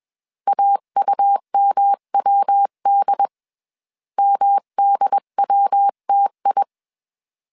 Morse code, heard as a keyed tone, about 21 wpm, 800 Hz. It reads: AUMWD MDWTI